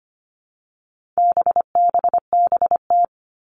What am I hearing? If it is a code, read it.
Morse code, heard as a keyed tone, about 25 wpm, 700 Hz. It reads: BBBT
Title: All Cut Numbers - 4-Digit 25wpm